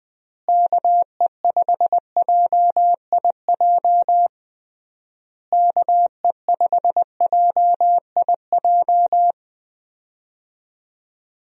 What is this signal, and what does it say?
Morse code, heard as a keyed tone, about 20 wpm, 700 Hz. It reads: KE5JIJ KE5JIJ
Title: USA Call Signs - 6 Characters - Repeated 3x - 20wpm